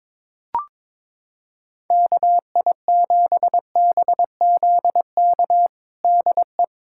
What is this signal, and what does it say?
Morse code, heard as a keyed tone, about 22 wpm, 700 Hz. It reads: KI7BZK DE
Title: USA Callsign DE Callsign - 6 Characters - 22wpm